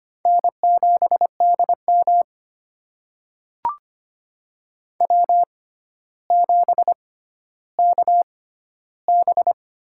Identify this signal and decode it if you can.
Morse code, heard as a keyed tone, about 25 wpm, spaced as for 13 wpm, 700 Hz. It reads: N7DM W7KB
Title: USA Call Signs - 4 Characters - 6x Character Spacing - 25wpm